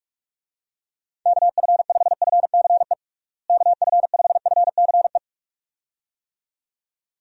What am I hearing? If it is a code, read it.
Morse code, heard as a keyed tone, about 46 wpm, 700 Hz. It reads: KF5FCE KF5FCE
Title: USA Call Signs - 6 Characters - Repeated 2x - 45wpm